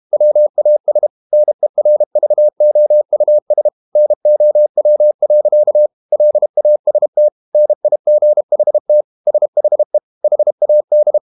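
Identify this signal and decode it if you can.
Morse code, heard as a keyed tone, about 32 wpm, 600 Hz. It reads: WAS NERVOUS NOW. LAST NIGHT SHE HAD
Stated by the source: Morse code practice at 32 wpm from CW med Gnister, reading Ken Follett's novel Never in English.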